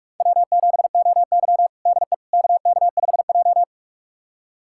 Morse code, heard as a keyed tone, about 42 wpm, 700 Hz. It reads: W7OY DE KK5J
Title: USA Callsign DE Callsign - 4 Characters - 45wpm